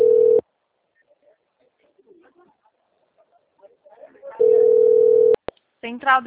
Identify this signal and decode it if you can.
Ringback tone (North America), 440 + 480 Hz, with speech.
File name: out- - - - - .31430